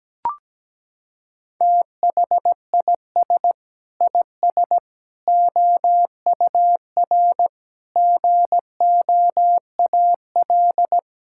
Morse code, heard as a keyed tone, about 17 wpm, 700 Hz. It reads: THIS IS OUR GOAL
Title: Sentences from Top 1000 Words - Review Part 5 - 17wpm